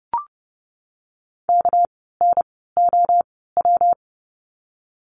Morse code, heard as a keyed tone, about 30 wpm, spaced as for 17 wpm, 700 Hz. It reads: KNOW
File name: course / 30:10wpm / 230 Lesson 37f - Letter x - Words 30wpm